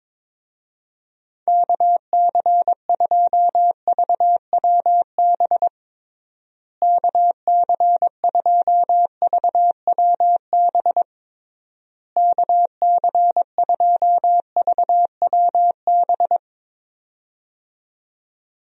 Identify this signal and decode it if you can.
Morse code, heard as a keyed tone, about 22 wpm, 700 Hz. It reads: KC2VWB KC2VWB KC2VWB
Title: USA Call Signs - 6 Characters - Repeated 3x - 22wpm